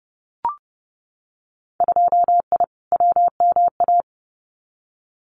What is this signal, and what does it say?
Morse code, heard as a keyed tone, about 30 wpm, 700 Hz. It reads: 2I WMA
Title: Winter Field Day Exchanges Encore 30wpm